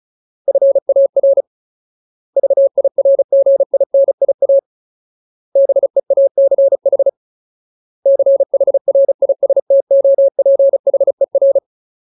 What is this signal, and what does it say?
Morse code, heard as a keyed tone, about 35 wpm, 550 Hz. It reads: FAR VIRGINIA BEACH CHRISTOPHER
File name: Ultimate Word List / 35WPM Ultimate